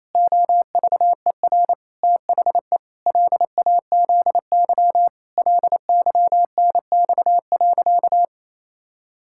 Morse code, heard as a keyed tone, about 28 wpm, 700 Hz. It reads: OVER THE LAZY LYNX.
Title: Pangram Challenge v2 - 28wpm